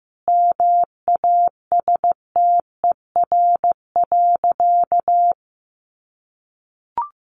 Morse code, heard as a keyed tone, about 15 wpm, 700 Hz. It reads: MASTER.
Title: Sentences from Top 1300 Words 15wpm